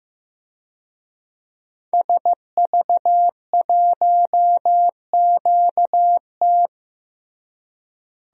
Morse code, heard as a keyed tone, about 15 wpm, 700 Hz. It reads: SV1QT
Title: DXCC-100 Call Signs - 5 Characters - Repeated 3x - 15wpm